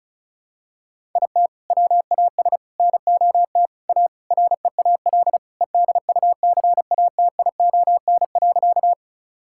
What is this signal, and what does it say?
Morse code, heard as a keyed tone, about 36 wpm, 700 Hz. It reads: IT WAS NOT A REAL EDUCATION.
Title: Sentences from Top 1000 Words - Review Part 6 - 35wpm